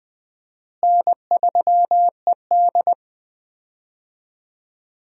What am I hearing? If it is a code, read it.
Morse code, heard as a keyed tone, about 20 wpm, 700 Hz. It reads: N3ED
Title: USA Call Signs - 4 Characters - Repeated 3x - 20wpm